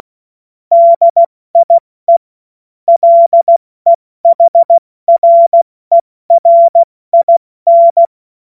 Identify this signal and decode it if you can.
Morse code, heard as a keyed tone, about 16 wpm, 680 Hz. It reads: DIE LEHRERIN
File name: Morsekurs / Witze / Witz 24. August 2021, 70 F80